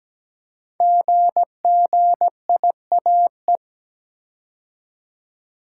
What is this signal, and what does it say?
Morse code, heard as a keyed tone, about 17 wpm, 700 Hz. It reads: GGIAE
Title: Code Groups - 5 Letters - 17wpm